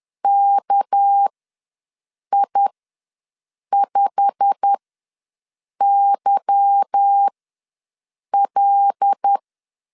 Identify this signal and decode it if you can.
Morse code, heard as a keyed tone, about 10 wpm, spaced as for 6 wpm, 800 Hz. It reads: KI5YL
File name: LEZIONI / 8 YJ45 / 8 YJ45 ws 10wpm 3843